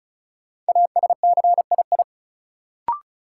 Morse code, heard as a keyed tone, about 36 wpm, 700 Hz. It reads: ASCII
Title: Sets of 5 Words - New Word Drill - 35wpm